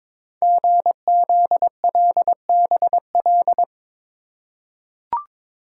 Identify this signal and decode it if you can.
Morse code, heard as a keyed tone, about 23 wpm, 700 Hz. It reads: GZLBL
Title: Code Groups - 5 Letters - 22wpm